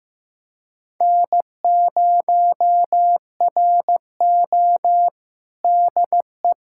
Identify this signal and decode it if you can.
Morse code, heard as a keyed tone, about 15 wpm, 700 Hz. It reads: N0RO DE
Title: USA Callsign DE Callsign 15wpm